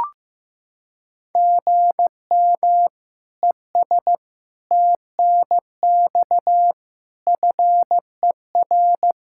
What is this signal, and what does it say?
Morse code, heard as a keyed tone, about 15 wpm, 700 Hz. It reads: GM ES TNX FER
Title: Practice QSO - Exchange 1 - Beginning - 15wpm